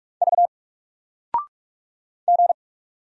Morse code, heard as a keyed tone, about 44 wpm, 700 Hz. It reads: V G
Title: Single Character (Letter:Number:Punctuation:Prosign) 45wpm